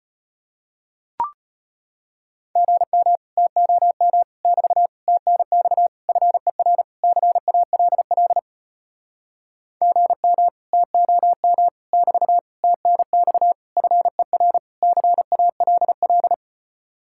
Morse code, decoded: GM TOM <BT> TNX FER CALL GM TOM <BT> TNX FER CALL

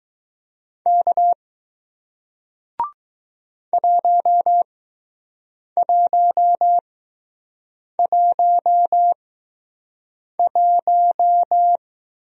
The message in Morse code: K 1111

